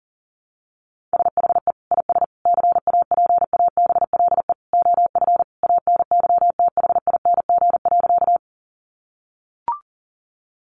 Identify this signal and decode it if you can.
Morse code, heard as a keyed tone, about 40 wpm, 700 Hz. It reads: SHE IS CAPABLE OF ANYTHING.
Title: Sentences from Top 1200 Words 40wpm